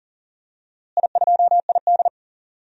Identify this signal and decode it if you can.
Morse code, heard as a keyed tone, about 38 wpm, 700 Hz. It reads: I2ID
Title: DXCC-100 Call Signs - 4 Characters - Repeated 3x - 40wpm